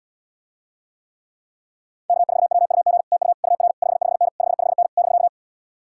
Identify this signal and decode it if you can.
Morse code, heard as a keyed tone, about 79 wpm, 700 Hz. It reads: K4WUG TU UR 55N 55N <BK>